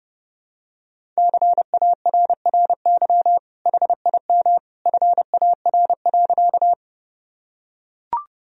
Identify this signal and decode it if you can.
Morse code, heard as a keyed tone, about 30 wpm, 700 Hz. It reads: CARRY HIM FAR.
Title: Sentences from Top 1000 Words - Review Part 1 - 30wpm